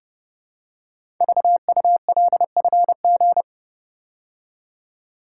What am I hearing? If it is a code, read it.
Morse code, heard as a keyed tone, about 29 wpm, 700 Hz. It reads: VULFG
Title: Code Groups - 5 Letters - 30wpm